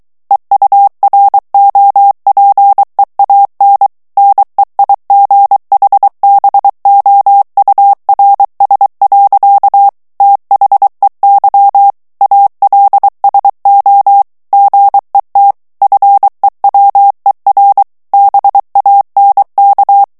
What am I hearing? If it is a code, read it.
Morse code, heard as a keyed tone, about 23 wpm, 800 Hz. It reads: EUROPEAN NEIGHBOURS. THEY ALSO GET FEWER BANK
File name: morse mp3 / ANGLAIS texte clair 1 vitesse 22 mots minute